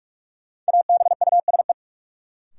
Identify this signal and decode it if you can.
Morse code, heard as a keyed tone, about 41 wpm, 700 Hz. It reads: ABUSE
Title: Sentences from Top 1300 Words - New Word Drill - 45wpm